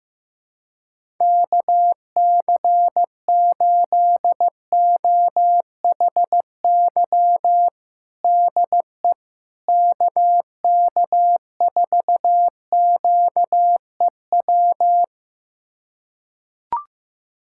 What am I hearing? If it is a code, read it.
Morse code, heard as a keyed tone, about 15 wpm, 700 Hz. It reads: KC8OHY DE KK4QEW
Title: USA Callsign DE Callsign 15wpm